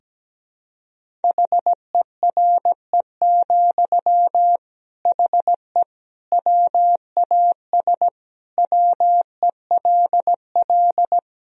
Morse code, heard as a keyed tone, about 17 wpm, 700 Hz. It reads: HERE, HE WAS WELL
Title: Sentences from Top 1000 Words - Review Part 3 - 17wpm